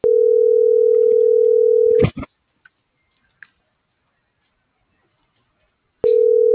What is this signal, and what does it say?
Ringback tone (North America), 440 + 480 Hz.